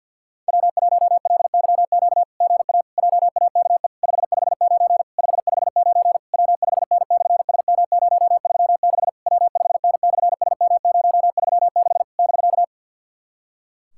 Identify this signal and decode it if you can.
Morse code, heard as a keyed tone, about 53 wpm, 700 Hz. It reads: W1LYQ GA JAKE 559 559 W5N/SM036 W5N/SM036 <BK>
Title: SOTA QSO Practice 50wpm